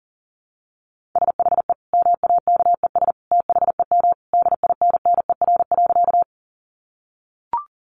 Morse code, heard as a keyed tone, about 37 wpm, 700 Hz. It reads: SHE MAKES THEM DINNER.
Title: Easy Phrases 40wpm